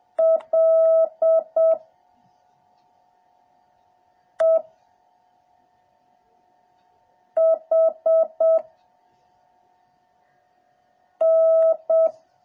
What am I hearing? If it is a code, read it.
Morse code, heard as a keyed tone, about 7 wpm, 1300 Hz. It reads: L E H N